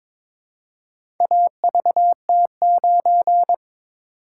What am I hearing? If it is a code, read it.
Morse code, heard as a keyed tone, about 22 wpm, 700 Hz. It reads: AVT9
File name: practice / 22wpm / Four Letter-Number Combinations - 22wpm